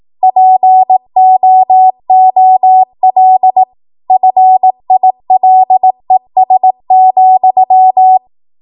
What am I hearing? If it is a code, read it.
Morse code, heard as a keyed tone, about 18 wpm, 750 Hz. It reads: POOL FILES,